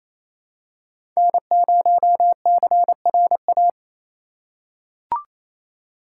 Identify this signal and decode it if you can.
Morse code, heard as a keyed tone, about 28 wpm, 690 Hz. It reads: N0CRA
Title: USA Call Signs - 5 Characters - 4x Character Spacing - 28wpm